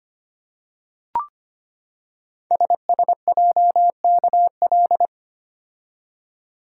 Morse code, heard as a keyed tone, about 25 wpm, 700 Hz. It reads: SSJKL
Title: Code Groups - 5 Letters - 25wpm